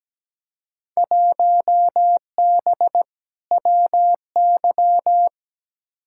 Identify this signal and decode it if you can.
Morse code, heard as a keyed tone, about 17 wpm, 700 Hz. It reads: 1B WY
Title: Field Day Exchanges Encore - 17wpm